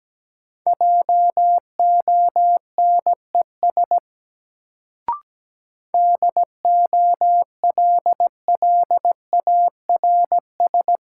Morse code, heard as a keyed tone, about 17 wpm, 700 Hz. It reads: JONES DOLLARS
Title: Sets of 5 Words - New Word Drill - 17wpm